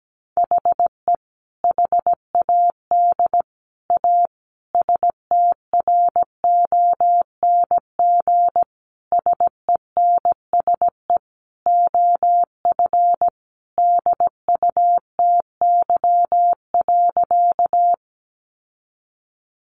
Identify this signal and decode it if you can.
Morse code, heard as a keyed tone, about 17 wpm, 700 Hz. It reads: HE HAD A STRONG SENSE OF DUTY.
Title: Sentences from Top 1500 Words 17wpm